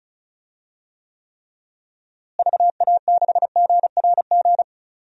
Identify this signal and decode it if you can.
Morse code, heard as a keyed tone, about 34 wpm, 700 Hz. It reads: VA6GRG
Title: DXCC-100 Call Signs - 6 Characters - Repeated 3x - 35wpm